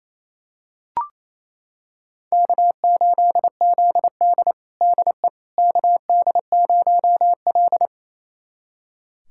Morse code, heard as a keyed tone, about 28 wpm, 700 Hz. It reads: K8ZD DE KD0L
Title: USA Callsign DE Callsign - 4 Characters - 28wpm